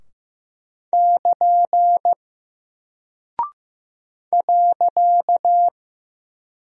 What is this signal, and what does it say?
Morse code, heard as a keyed tone, about 15 wpm, 700 Hz. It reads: <KN> .
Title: Single Character (Letter:Number:Punctuation:Prosign) Alt 15wpm